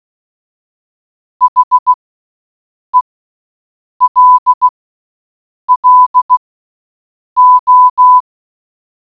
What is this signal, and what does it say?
Morse code, heard as a keyed tone, about 16 wpm, spaced as for 7 wpm, 1000 Hz. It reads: HELLO